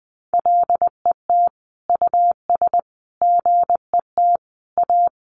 Morse code, heard as a keyed tone, about 19 wpm, 700 Hz. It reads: LET US GET A
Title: Sentences from Top 1400 Words 20wpm